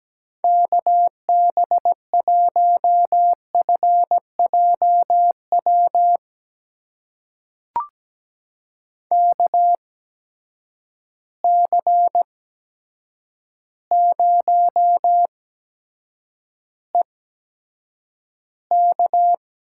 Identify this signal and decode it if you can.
Morse code, heard as a keyed tone, about 17 wpm, spaced as for 8 wpm, 700 Hz. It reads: KB1FJW KC0EK
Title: USA Call Signs - 6 Characters - 8x Character Spacing - 17wpm